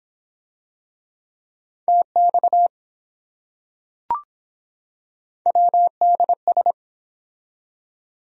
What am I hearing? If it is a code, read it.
Morse code, heard as a keyed tone, about 23 wpm, 700 Hz. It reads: TX WDS